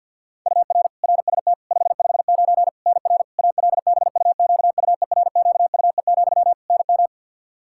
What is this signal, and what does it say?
Morse code, heard as a keyed tone, about 49 wpm, 700 Hz. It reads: UR RST 559 NR ALBUQUERQUE, NM
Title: Practice QSO - Exchange 1 - Beginning - 50wpm